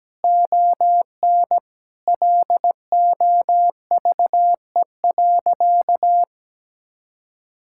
Morse code, heard as a keyed tone, about 17 wpm, 700 Hz. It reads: ON LOVE.
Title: Sentences from Top 1000 Words - Review Part 6 - 17wpm